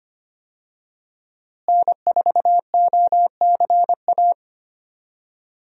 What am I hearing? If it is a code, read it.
Morse code, heard as a keyed tone, about 25 wpm, 700 Hz. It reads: N4OCA
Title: USA Call Signs - 5 Characters - Repeated 3x - 25wpm